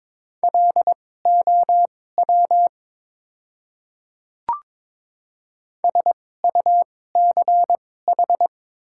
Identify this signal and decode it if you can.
Morse code, heard as a keyed tone, about 22 wpm, spaced as for 16 wpm, 700 Hz. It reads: LOW SUCH
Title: 2x Character Spacing - Top 300 Words - 22wpm